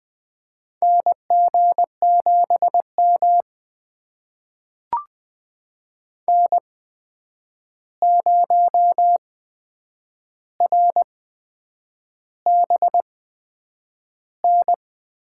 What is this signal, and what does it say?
Morse code, heard as a keyed tone, about 20 wpm, spaced as for 8 wpm, 700 Hz. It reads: NG7M N0RBN
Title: USA Call Signs - 8x Character Spacing - 20wpm